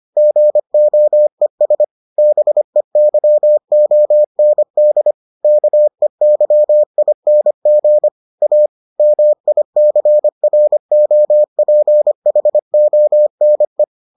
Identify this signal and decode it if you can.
Morse code, heard as a keyed tone, about 25 wpm, 600 Hz. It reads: GOES BEYOND KEYING A MICROPHONE